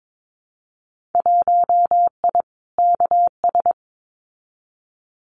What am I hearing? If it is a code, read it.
Morse code, heard as a keyed tone, about 22 wpm, 700 Hz. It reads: 1I KS
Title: Winter Field Day Exchanges Encore 22wpm